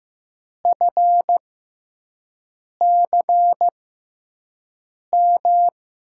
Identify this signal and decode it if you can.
Morse code, heard as a keyed tone, about 15 wpm, spaced as for 5 wpm, 700 Hz. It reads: FCM
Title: USA Call Signs - 6x Character Spacing - 15wpm